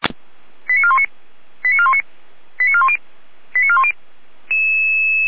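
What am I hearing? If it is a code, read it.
Five-tone selective call (ZVEI), heard as quick runs of five tones, tones of about 70 ms. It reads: 89419 89419 89410 89410
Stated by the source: Five-tone ZVEI selective call 89165 received by SDR.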